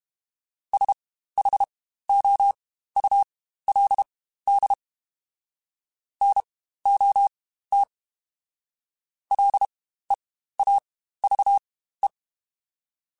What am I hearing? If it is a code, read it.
Morse code, heard as a keyed tone, about 32 wpm, spaced as for 15 wpm, 800 Hz. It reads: SHOULD NOT LEAVE
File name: Spaced / Chuckles Spaced-mp3s / Temptation sp @30